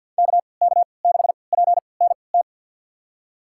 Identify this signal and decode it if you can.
Morse code, heard as a keyed tone, about 52 wpm, spaced as for 29 wpm, 700 Hz. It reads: KK6PNT
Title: USA Call Signs - 6 Characters - 3x Character Spacing - 50wpm